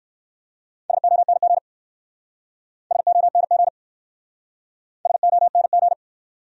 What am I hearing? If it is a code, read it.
Morse code, heard as a keyed tone, about 61 wpm, 700 Hz. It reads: SONG SONG SONG